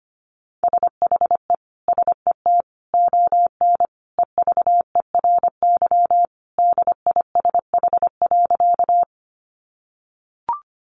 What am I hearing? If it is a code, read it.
Morse code, heard as a keyed tone, about 25 wpm, 700 Hz. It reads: SHE SET ON EVERY DISH.
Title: Sentences from Top 1300 Words 25wpm